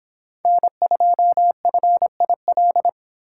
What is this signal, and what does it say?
Morse code, heard as a keyed tone, about 24 wpm, 700 Hz. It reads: N2FIL